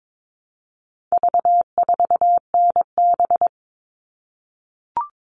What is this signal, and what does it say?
Morse code, heard as a keyed tone, about 22 wpm, 700 Hz. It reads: V4NB
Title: All Cut Numbers - 4-Digit 22wpm